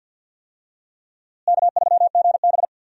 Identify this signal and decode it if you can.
Morse code, heard as a keyed tone, about 53 wpm, 700 Hz. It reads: K3GB
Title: USA Call Signs - 4 Characters - Repeated 3x - 50wpm